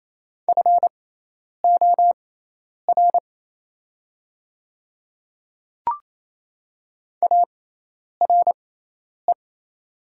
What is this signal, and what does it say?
Morse code, heard as a keyed tone, about 27 wpm, spaced as for 10 wpm, 700 Hz. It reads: FOR ARE
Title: 6x Character Spacing - Top 100 Words - 28wpm